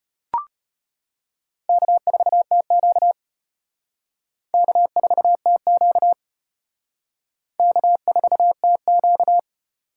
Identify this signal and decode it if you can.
Morse code, heard as a keyed tone, about 34 wpm, 700 Hz. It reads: K4TQ K4TQ K4TQ